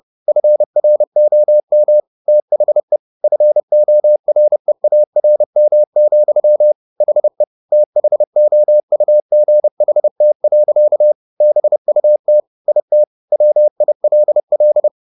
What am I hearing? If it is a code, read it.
Morse code, heard as a keyed tone, about 30 wpm, 600 Hz. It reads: FROM THE FOREARM, HE THOUGHT. BUT IT WILL
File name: the old man and the sea / the old man and the sea 1 30wpm